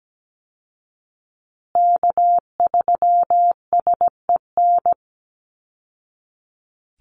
Morse code, heard as a keyed tone, about 17 wpm, 700 Hz. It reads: K3SEN
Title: Call Signs Repeated 3x - 17wpm